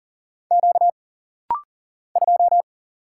Morse code, heard as a keyed tone, about 40 wpm, 700 Hz. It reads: Q 2
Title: Single Letter-Number - Rapid-Fire 40wpm